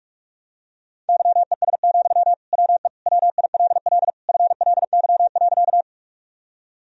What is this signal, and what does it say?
Morse code, heard as a keyed tone, about 46 wpm, 700 Hz. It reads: YES, WE WILL FLY.
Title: Sentences from Top 1000 Words - Review Part 2 - 45wpm